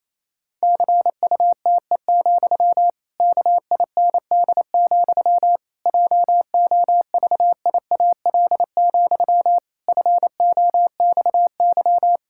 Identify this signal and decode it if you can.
Morse code, heard as a keyed tone, about 28 wpm, 700 Hz. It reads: CUTE, KIND, JOVIAL, FOXY
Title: Pangram Challenge v2 - 28wpm